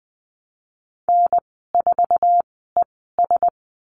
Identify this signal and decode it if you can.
Morse code, heard as a keyed tone, about 19 wpm, spaced as for 14 wpm, 700 Hz. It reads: N4ES
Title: 048 Lesson 9g - Number 4 - Call Signs 20wpm